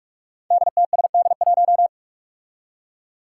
Morse code, heard as a keyed tone, about 41 wpm, 700 Hz. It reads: DTSD1
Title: Five Challenging Letter-Number Combinations 45wpm